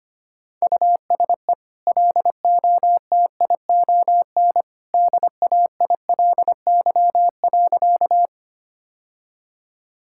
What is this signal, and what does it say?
Morse code, heard as a keyed tone, about 25 wpm, 700 Hz. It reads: USE LOTION DAILY.
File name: Easy Sentences with Hints - 25wpm